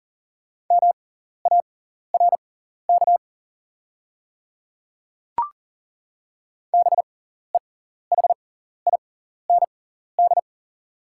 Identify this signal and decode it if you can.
Morse code, heard as a keyed tone, about 40 wpm, spaced as for 14 wpm, 700 Hz. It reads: MARK BEHIND
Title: 6x Character Spacing - Top 500 Words - 40wpm